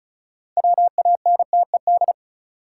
Morse code, heard as a keyed tone, about 36 wpm, 700 Hz. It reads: WANTED
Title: Sets of 2 Words - New Word Drill - 35wpm